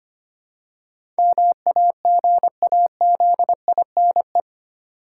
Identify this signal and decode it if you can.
Morse code, heard as a keyed tone, about 25 wpm, 700 Hz. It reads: MAGAZINE
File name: 8 Letter Words - First 3 Letters Given - 25wpm